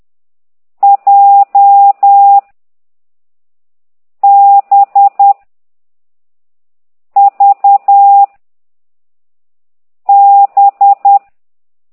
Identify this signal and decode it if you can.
Morse code, heard as a keyed tone, about 10 wpm, 800 Hz. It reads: J B V B
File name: exerc7